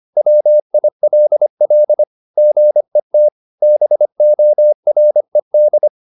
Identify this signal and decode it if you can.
Morse code, heard as a keyed tone, about 25 wpm, 600 Hz. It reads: WILL GET BORED